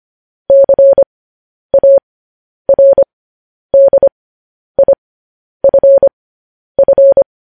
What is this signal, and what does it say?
Morse code, heard as a keyed tone, about 25 wpm, spaced as for 10 wpm, 550 Hz. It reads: CARDIFF
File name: podcast / News headlines-1x-25x10